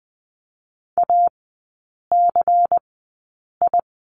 Morse code, heard as a keyed tone, about 20 wpm, spaced as for 8 wpm, 700 Hz. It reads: ACI
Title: 093 Lesson 16e - Letter c - Three Characters 20wpm